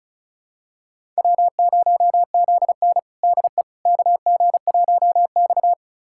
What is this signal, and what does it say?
Morse code, heard as a keyed tone, about 34 wpm, 700 Hz. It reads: W0ZN DE KG1X
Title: USA Callsign DE Callsign - 4 Characters - 35wpm